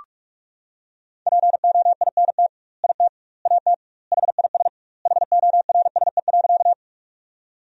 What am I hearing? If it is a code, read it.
Morse code, heard as a keyed tone, about 41 wpm, 700 Hz. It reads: POINT IT AT HIS HORSE.
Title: 2x Word-Spacing - Sentences from Top 300 Words - 45wpm